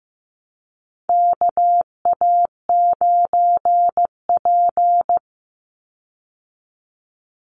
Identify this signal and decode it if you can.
Morse code, heard as a keyed tone, about 15 wpm, 700 Hz. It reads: KA9P
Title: Call Signs Repeated 2x - 15wpm